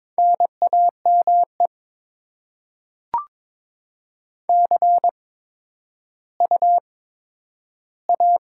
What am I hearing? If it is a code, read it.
Morse code, heard as a keyed tone, about 22 wpm, spaced as for 13 wpm, 700 Hz. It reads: NAME CU A
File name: Top 20 QSO Elements - 8x Character Spacing - 22wpm